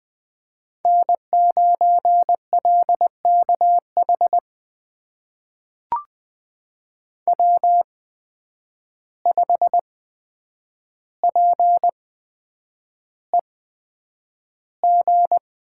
Morse code, heard as a keyed tone, about 20 wpm, spaced as for 9 wpm, 700 Hz. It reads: N9LKH W5PEG
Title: USA Call Signs - 5 Characters - 8x Character Spacing - 20wpm